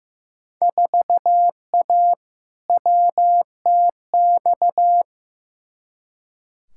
Morse code, heard as a keyed tone, about 15 wpm, 700 Hz. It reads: 4A WTX
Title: Field Day Exchanges Encore - 15wpm